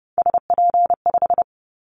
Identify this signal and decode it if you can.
Morse code, heard as a keyed tone, about 32 wpm, 700 Hz. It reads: SP5
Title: Callsign Prefixes for Top 25 DXCC - 30wpm